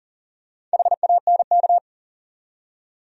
Morse code, heard as a keyed tone, about 40 wpm, 700 Hz. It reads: HANK
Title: USA Common Names - Max 4 Letters - 40wpm